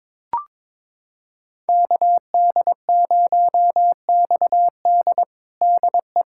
Morse code, heard as a keyed tone, about 22 wpm, 700 Hz. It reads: KD0XD DE
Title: USA Callsign DE Callsign - 5 Characters - 22wpm